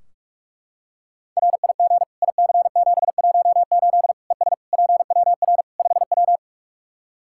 Morse code, heard as a keyed tone, about 41 wpm, 700 Hz. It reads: RIG IC718 ES PWR 5W
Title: Practice QSO - Exchange 2 - Info - 45wpm